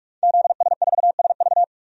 Morse code, heard as a keyed tone, about 46 wpm, 700 Hz. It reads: ZS4SV
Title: DXCC-100 Call Signs - 5 Characters - 45wpm